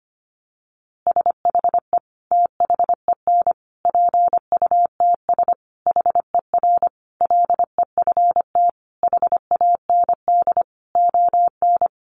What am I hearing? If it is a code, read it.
Morse code, heard as a keyed tone, about 25 wpm, 700 Hz. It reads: SHE THEN PUTS HER LEFT HAND ON
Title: Easy Phrases 25wpm